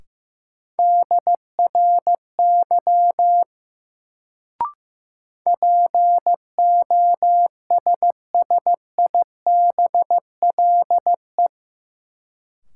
Morse code, heard as a keyed tone, about 15 wpm, 700 Hz. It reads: DRY POSSIBLE
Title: Sentences from Top 500 Words - New Word Drill - 15wpm